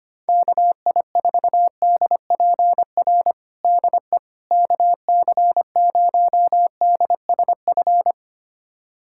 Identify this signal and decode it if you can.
Morse code, heard as a keyed tone, about 25 wpm, 700 Hz. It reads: KI4DPR DE KC0DSF
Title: USA Callsign DE Callsign 25wpm